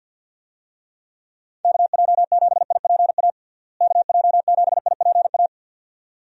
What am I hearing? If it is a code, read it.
Morse code, heard as a keyed tone, about 52 wpm, 700 Hz. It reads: KJ7IPA KJ7IPA
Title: USA Call Signs - 6 Characters - Repeated 2x - 50wpm